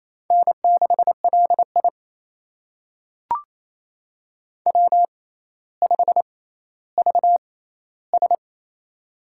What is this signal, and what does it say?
Morse code, heard as a keyed tone, about 28 wpm, spaced as for 14 wpm, 700 Hz. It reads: N6LI W5VS
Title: USA Call Signs - 4 Characters - 6x Character Spacing - 28wpm